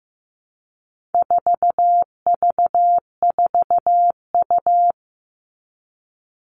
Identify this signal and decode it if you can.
Morse code, heard as a keyed tone, about 15 wpm, 700 Hz. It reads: 4V4U
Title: All Cut Numbers - 4-Digit 15wpm